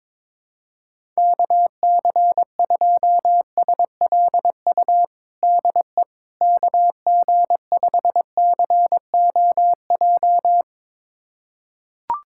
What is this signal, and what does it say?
Morse code, heard as a keyed tone, about 22 wpm, 700 Hz. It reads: KC2SLU DE KG5COJ
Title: USA Callsign DE Callsign - 6 Characters - 22wpm